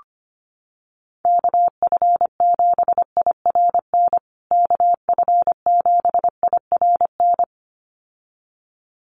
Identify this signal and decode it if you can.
Morse code, heard as a keyed tone, about 25 wpm, 700 Hz. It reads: KF7IRN KF7IRN
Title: USA Call Signs Repeated 2x 25wpm